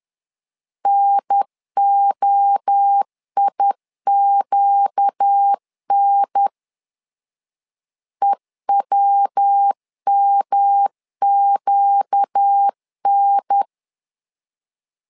Morse code, heard as a keyed tone, about 10 wpm, 800 Hz. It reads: NOIQN EWMQN
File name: LEZIONI / 4 Q 10wpm 2210